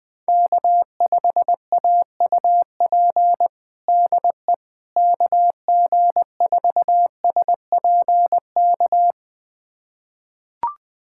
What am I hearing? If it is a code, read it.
Morse code, heard as a keyed tone, about 20 wpm, 700 Hz. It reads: K5AUP DE KG4SPK
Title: USA Callsign DE Callsign 20wpm